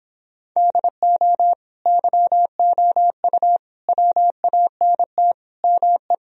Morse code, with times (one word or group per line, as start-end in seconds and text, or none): 0.6-1.5 DO
1.9-3.6 YOU
3.9-5.3 WANT
5.6-6.1 ME